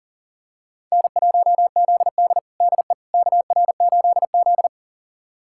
Morse code, heard as a keyed tone, about 37 wpm, 700 Hz. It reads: N1ZD DE KR8Z
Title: USA Callsign DE Callsign - 4 Characters - 40wpm